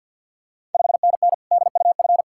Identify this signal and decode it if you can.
Morse code, heard as a keyed tone, about 51 wpm, 690 Hz. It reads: 5NN DUF
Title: Ontario QSO Party Exchange 50wpm